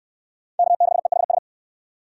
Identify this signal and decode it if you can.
Morse code, heard as a keyed tone, about 68 wpm, 700 Hz. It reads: K6EHS